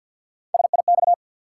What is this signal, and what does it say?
Morse code, heard as a keyed tone, about 52 wpm, 700 Hz. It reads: SIX
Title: Sentences from Top 600 Words - New Word Drill - 50wpm